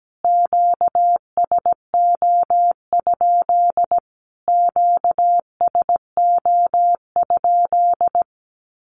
Morse code, heard as a keyed tone, about 17 wpm, 700 Hz. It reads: QSO? QSO?
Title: Q-Codes Repeated 3x 17wpm